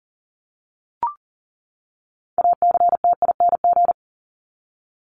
Morse code, heard as a keyed tone, about 40 wpm, 700 Hz. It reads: ACTING
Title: ING Suffix - Top 100 Words with Base Words 40wpm